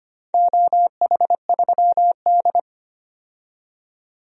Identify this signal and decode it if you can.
Morse code, heard as a keyed tone, about 25 wpm, 700 Hz. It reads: OH3D
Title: DXCC-100 Call Signs - 4 Characters - 25wpm